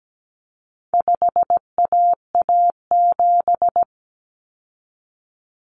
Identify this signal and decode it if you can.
Morse code, heard as a keyed tone, about 17 wpm, 700 Hz. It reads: 5AA7
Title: Common Cut Numbers - 4-Digit 17wpm